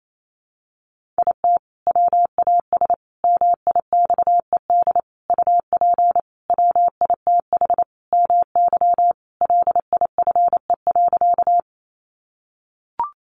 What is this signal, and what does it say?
Morse code, heard as a keyed tone, about 28 wpm, 700 Hz. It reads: IT WAS MIXED UP WITH MY LIFE.
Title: Sentences from Top 1500 Words 28wpm